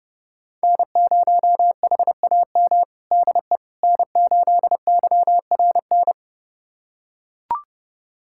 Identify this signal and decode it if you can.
Morse code, heard as a keyed tone, about 32 wpm, 700 Hz. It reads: N0HAM DE N8YRN
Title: USA Callsign DE Callsign - 5 Characters - 30wpm